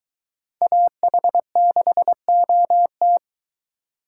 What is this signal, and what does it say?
Morse code, heard as a keyed tone, about 23 wpm, 700 Hz. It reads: AH6OT